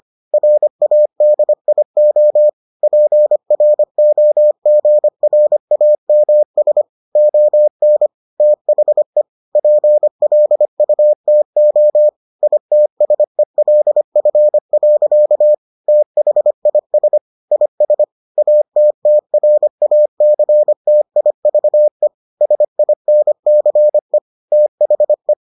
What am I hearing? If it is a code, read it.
Morse code, heard as a keyed tone, about 25 wpm, 600 Hz. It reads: RADIO PROGRAMS ON THE PLUTO ITSELF. THIS IS ATTRACTIVE SINCE THE